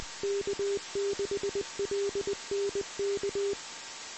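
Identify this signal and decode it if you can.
Morse code, heard as a keyed tone, about 20 wpm, 400 Hz, with noise. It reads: K6LNK
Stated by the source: Morse code recorded on the 2 m band at 146.115 MHz.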